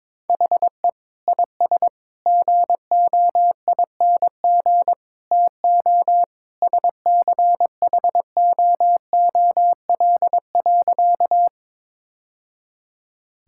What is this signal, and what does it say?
Morse code, heard as a keyed tone, about 22 wpm, 700 Hz. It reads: HE IS GOING TO SCHOOL.
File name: practice / 22wpm / Sentences with 3-Word Phrases - 22wpm